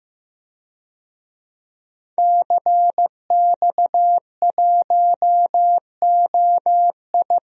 Morse code, heard as a keyed tone, about 15 wpm, 700 Hz. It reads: CX1OI
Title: DXCC-100 Call Signs - 5 Characters - Repeated 2x - 15wpm